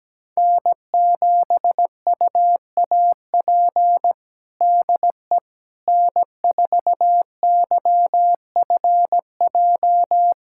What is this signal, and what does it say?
Morse code, heard as a keyed tone, about 17 wpm, 700 Hz. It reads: N7UAP DE N4YFJ
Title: USA Callsign DE Callsign 17wpm